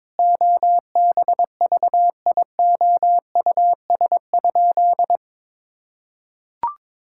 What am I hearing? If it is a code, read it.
Morse code, heard as a keyed tone, about 22 wpm, 700 Hz. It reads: OBVIOUS?
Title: Sentences from Top 1600 Words 22wpm